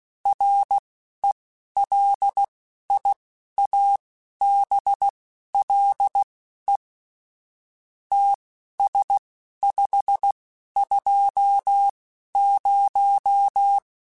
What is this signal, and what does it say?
Morse code, heard as a keyed tone, about 16 wpm, spaced as for 11 wpm, 800 Hz. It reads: RELIABLE TS520
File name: QSOs / QSOs-mp3s / WD0FBT 1@15